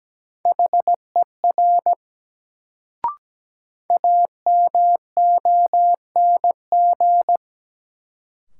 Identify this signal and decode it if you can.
Morse code, heard as a keyed tone, about 17 wpm, 700 Hz. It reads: HER AMONG
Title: Sets of 2 Words - New Word Drill - 17wpm